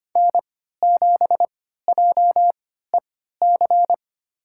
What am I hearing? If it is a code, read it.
Morse code, heard as a keyed tone, about 25 wpm, spaced as for 14 wpm, 700 Hz. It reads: N7JEC
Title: USA Call Signs - 5 Characters - 3x Character Spacing - 25wpm